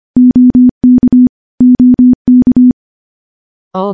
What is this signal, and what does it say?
Morse code, heard as a keyed tone, about 25 wpm, 250 Hz. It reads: OK OK
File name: morse / morse-training-25WPM